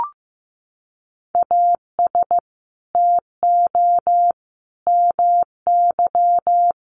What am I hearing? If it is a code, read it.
Morse code, heard as a keyed tone, about 15 wpm, 700 Hz. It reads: AS TO MY
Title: Sentences from Top 600 Words 15wpm